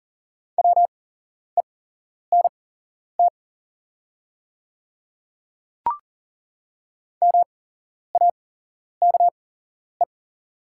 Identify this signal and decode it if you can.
Morse code, heard as a keyed tone, about 40 wpm, spaced as for 11 wpm, 700 Hz. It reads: WENT MAKE
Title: 8x Character Spacing - Top 200 Words - 40wpm